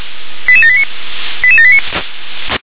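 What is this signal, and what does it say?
Five-tone selective call (ZVEI), heard as quick runs of five tones, tones of about 70 ms. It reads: 88789 88789
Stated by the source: ZVEI five-tone call received on SDR.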